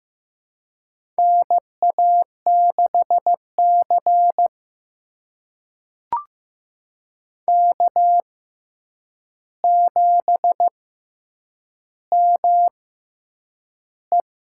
Morse code, heard as a keyed tone, about 15 wpm, spaced as for 8 wpm, 700 Hz. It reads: NA6C K7ME
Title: USA Call Signs - 4 Characters - 6x Character Spacing - 15wpm